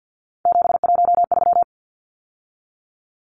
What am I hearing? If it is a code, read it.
Morse code, heard as a keyed tone, about 49 wpm, 700 Hz. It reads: 713
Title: Numbers - 3-Digit 50wpm